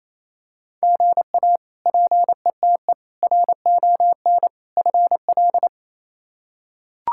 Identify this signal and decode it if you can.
Morse code, heard as a keyed tone, about 28 wpm, 700 Hz. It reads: GA PETE RON FL
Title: K1USN SST QSO Practice - 28wpm